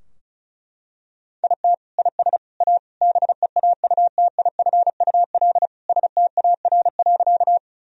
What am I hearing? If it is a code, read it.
Morse code, heard as a keyed tone, about 34 wpm, 700 Hz. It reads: IT IS A BEAUTIFUL STAR.
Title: Sentences from Top 1000 Words - Review Part 2 - 35wpm